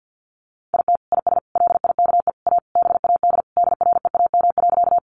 Morse code, heard as a keyed tone, about 50 wpm, 700 Hz. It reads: IT IS LIKE A BAD DREAM.
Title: Sentences from Top 1500 Words 50wpm